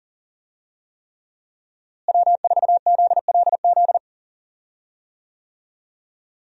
Morse code, heard as a keyed tone, about 40 wpm, 700 Hz. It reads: W4ZLZ
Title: USA Call Signs - 5 Characters - Repeated 3x - 40wpm